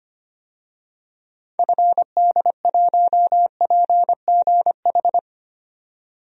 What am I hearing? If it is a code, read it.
Morse code, heard as a keyed tone, about 25 wpm, 700 Hz. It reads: FD1PGH
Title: DXCC-100 Call Signs - 6 Characters - 25wpm